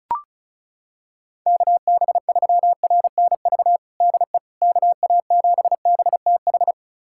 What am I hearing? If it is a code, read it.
Morse code, heard as a keyed tone, about 34 wpm, 700 Hz. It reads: KB3RNV DE KA7BTH
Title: USA Callsign DE Callsign 35wpm